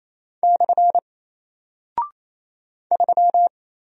Morse code, heard as a keyed tone, about 27 wpm, 700 Hz. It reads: / 3
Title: Single Character (Letter:Number:Punctuation:Prosign) Alt 28wpm